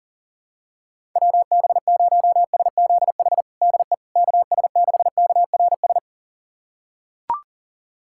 Morse code, heard as a keyed tone, about 40 wpm, 700 Hz. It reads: WB0SZH DE KS6KRS
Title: USA Callsign DE Callsign 40wpm